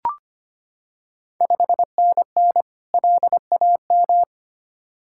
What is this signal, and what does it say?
Morse code, heard as a keyed tone, about 25 wpm, 690 Hz. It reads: 5NN LAM
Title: Ontario QSO Party Exchange 25wpm